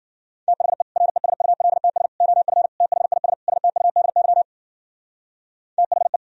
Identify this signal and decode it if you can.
Morse code, heard as a keyed tone, about 50 wpm, 700 Hz. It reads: THE RESULTS OF THIS STUDY THE
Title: Speed-Racing - Sets of 5 Words 40wpm